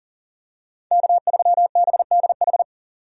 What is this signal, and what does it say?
Morse code, heard as a keyed tone, about 40 wpm, 700 Hz. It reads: K3BDH
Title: USA Call Signs - 5 Characters - 40wpm